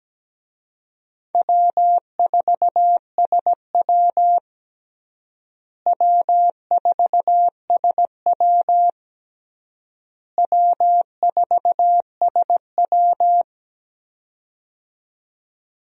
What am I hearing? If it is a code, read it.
Morse code, heard as a keyed tone, about 17 wpm, 700 Hz. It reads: W4SW W4SW W4SW
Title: USA Call Signs - 4 Characters - Repeated 3x - 17wpm